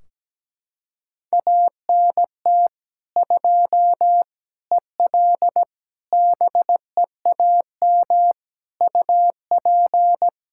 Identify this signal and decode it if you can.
Morse code, heard as a keyed tone, about 17 wpm, 700 Hz. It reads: ANT 2 EL BEAM UP
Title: Practice QSO - Exchange 2 - Info - 17wpm